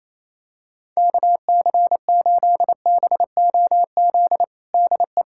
Morse code, heard as a keyed tone, about 28 wpm, 700 Hz. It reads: KC8BOZ DE
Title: USA Callsign DE Callsign - 6 Characters - 28wpm